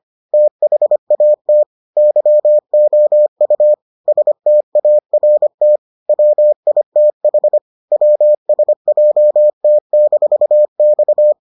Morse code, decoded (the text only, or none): THAT YOU START WITH WSJT-X